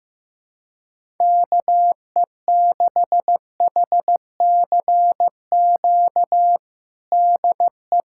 Morse code, heard as a keyed tone, about 15 wpm, 700 Hz. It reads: KE6HCQ DE
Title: USA Callsign DE Callsign 15wpm